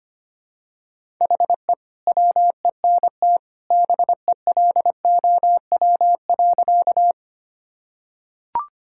Morse code, heard as a keyed tone, about 25 wpm, 700 Hz. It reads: HE WENT BELOW.
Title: Sentences from Top 1000 Words - Review Part 5 - 25wpm